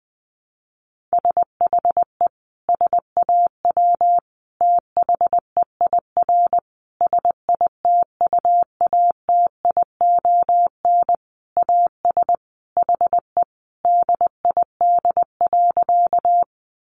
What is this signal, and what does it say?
Morse code, heard as a keyed tone, about 20 wpm, 700 Hz. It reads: SHE SAW THEIR SITUATION AS HE DID.
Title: Sentences from Top 1200 Words 20wpm